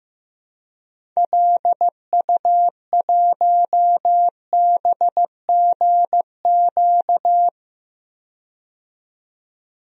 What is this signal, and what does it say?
Morse code, heard as a keyed tone, about 15 wpm, 700 Hz. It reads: LU1BGQ
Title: DXCC-100 Call Signs - 6 Characters - 15wpm